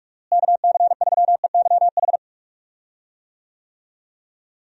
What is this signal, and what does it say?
Morse code, heard as a keyed tone, about 46 wpm, 700 Hz. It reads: KC3EYH
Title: USA Call Signs - 6 Characters - Repeated 2x - 45wpm